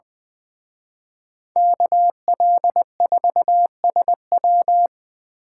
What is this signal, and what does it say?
Morse code, heard as a keyed tone, about 25 wpm, 700 Hz. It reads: KL4SW
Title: Speed Racing - USA 2x2 Call Signs 20wpm